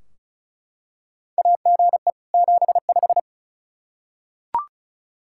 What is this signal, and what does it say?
Morse code, heard as a keyed tone, about 36 wpm, 700 Hz. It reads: AGE 75
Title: Practice QSO - Exchange 3 - More Info - 35wpm